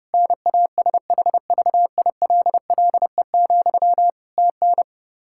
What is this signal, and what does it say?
Morse code, heard as a keyed tone, about 30 wpm, 700 Hz. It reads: NASHVILLE, TN
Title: Practice QSO - Exchange 1 - Beginning - 30wpm